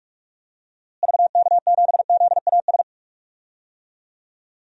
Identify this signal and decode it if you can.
Morse code, heard as a keyed tone, about 47 wpm, 700 Hz. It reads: VK7ZAS
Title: DXCC-100 Call Signs - 6 Characters - Repeated 2x - 45wpm